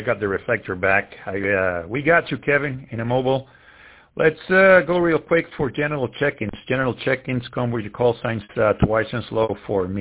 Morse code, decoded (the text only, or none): none